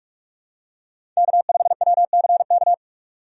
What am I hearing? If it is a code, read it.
Morse code, heard as a keyed tone, about 46 wpm, 700 Hz. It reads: K5WCK
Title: USA Call Signs - 5 Characters - 45wpm